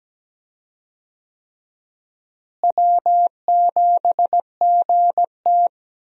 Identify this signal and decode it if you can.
Morse code, heard as a keyed tone, about 17 wpm, 700 Hz. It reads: W7GT
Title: USA Call Signs - 4 Characters - Repeated 3x - 17wpm